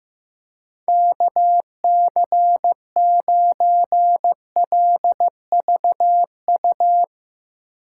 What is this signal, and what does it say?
Morse code, heard as a keyed tone, about 16 wpm, 700 Hz. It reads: KC9LVU
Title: USA Call Signs - 6 Characters - 3x Character Spacing - 15wpm